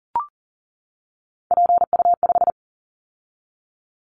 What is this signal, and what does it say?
Morse code, heard as a keyed tone, about 42 wpm, 700 Hz. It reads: PU5
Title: Callsign Prefixes for Top 50 DXCC - 40wpm